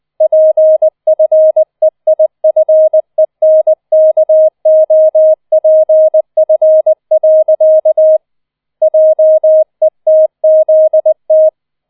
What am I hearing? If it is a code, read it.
Morse code, heard as a keyed tone, about 20 wpm, 630 Hz. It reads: PFEIFENKOPF. JETZT